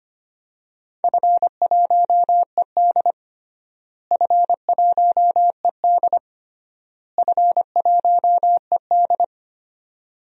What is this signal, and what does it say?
Morse code, heard as a keyed tone, about 25 wpm, 700 Hz. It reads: F1ED F1ED F1ED
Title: DXCC-100 Call Signs - 4 Characters - Repeated 3x - 25wpm